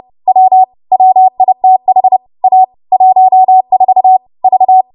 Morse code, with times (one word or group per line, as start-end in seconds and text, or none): 0.3-0.6 W
0.9-2.2 WITH
2.4-2.6 A
2.9-4.2 14
4.4-4.8 V